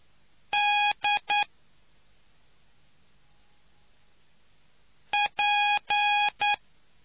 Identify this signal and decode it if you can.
Morse code, heard as a keyed tone, about 9 wpm, 830 Hz. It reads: D P